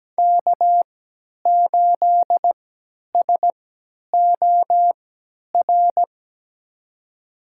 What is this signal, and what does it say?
Morse code, heard as a keyed tone, about 17 wpm, spaced as for 10 wpm, 700 Hz. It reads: K8SOR
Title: USA Call Signs - 5 Characters - 3x Character Spacing - 17wpm